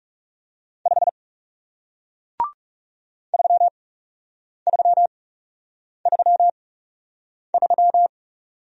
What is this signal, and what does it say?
Morse code, heard as a keyed tone, about 40 wpm, spaced as for 8 wpm, 700 Hz. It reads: 5 3333